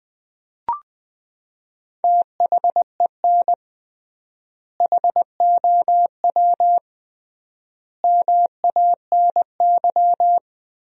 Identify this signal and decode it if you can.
Morse code, heard as a keyed tone, about 20 wpm, 700 Hz. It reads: THEN HOW MANY
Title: 3x Word-Spacing - Sentences from Top 100 Words - 20wpm